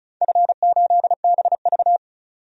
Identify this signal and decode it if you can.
Morse code, heard as a keyed tone, about 36 wpm, 700 Hz. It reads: F8BV
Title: DXCC-100 Call Signs - 4 Characters - 35wpm